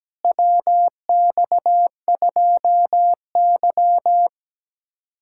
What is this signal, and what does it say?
Morse code, heard as a keyed tone, about 17 wpm, 700 Hz. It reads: WX2Y
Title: USA Callsign DE Callsign - 4 Characters - 17wpm